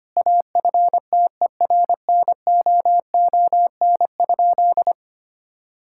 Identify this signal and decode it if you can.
Morse code, heard as a keyed tone, about 25 wpm, 700 Hz. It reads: AFTERNOON?
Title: Sentences from Top 1600 Words 25wpm